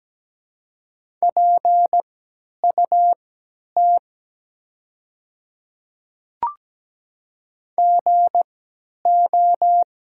Morse code, heard as a keyed tone, about 17 wpm, spaced as for 10 wpm, 700 Hz. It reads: PUT GO
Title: 3x Character Spacing - Top 200 Words - 17wpm